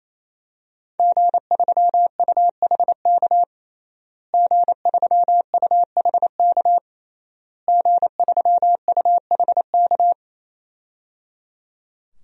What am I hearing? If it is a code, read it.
Morse code, heard as a keyed tone, about 28 wpm, 700 Hz. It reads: G3UHK G3UHK G3UHK
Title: DXCC-100 Call Signs - 5 Characters - Repeated 3x - 28wpm